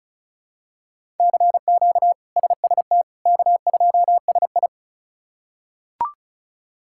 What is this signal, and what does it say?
Morse code, heard as a keyed tone, about 34 wpm, 700 Hz. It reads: CQ SST K2SI
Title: K1USN SST QSO Practice - 35wpm